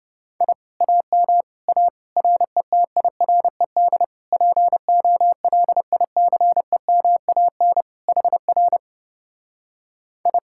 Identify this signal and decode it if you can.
Morse code, heard as a keyed tone, about 29 wpm, 700 Hz. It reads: I AM A RETIRED POLICEMAN HR I